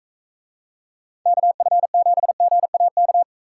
Morse code, heard as a keyed tone, about 44 wpm, 700 Hz. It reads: KF7GAK